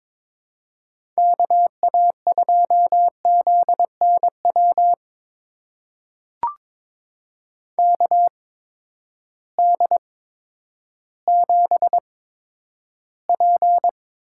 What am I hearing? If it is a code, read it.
Morse code, heard as a keyed tone, about 22 wpm, spaced as for 11 wpm, 700 Hz. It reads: KA2ZNW KD7P
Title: USA Call Signs - 6 Characters - 8x Character Spacing - 22wpm